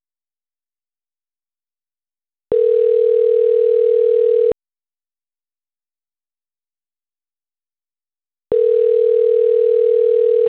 Ringback tone (North America), 440 + 480 Hz.